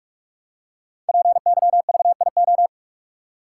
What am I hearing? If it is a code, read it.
Morse code, heard as a keyed tone, about 44 wpm, 700 Hz. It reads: PYVIO
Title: Code Groups - 5 Letter:Numbers - 45wpm